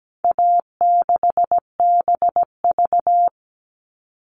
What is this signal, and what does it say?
Morse code, heard as a keyed tone, about 17 wpm, 700 Hz. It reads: A6BV
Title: All Cut Numbers - 4-Digit 17wpm